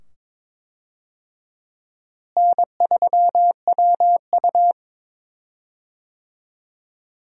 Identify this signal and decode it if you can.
Morse code, heard as a keyed tone, about 23 wpm, 700 Hz. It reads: N3WU
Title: USA Call Signs - 4 Characters - Repeated 2x - 22wpm